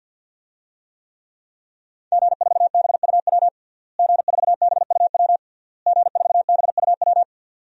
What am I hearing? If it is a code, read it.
Morse code, heard as a keyed tone, about 51 wpm, 700 Hz. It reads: G4BUW G4BUW G4BUW
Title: DXCC-100 Call Signs - 5 Characters - Repeated 3x - 50wpm